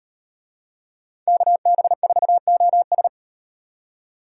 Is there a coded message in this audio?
Yes: Morse code from 1.3 to 3.1 s, heard as a keyed tone, about 38 wpm, 700 Hz. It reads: KB4OS